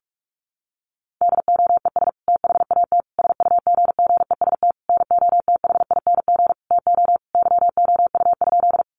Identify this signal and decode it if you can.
Morse code, heard as a keyed tone, about 44 wpm, 700 Hz. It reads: DOES THAT SUGGEST NOTHING TO YOU?
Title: Sentences from Top 1300 Words 45wpm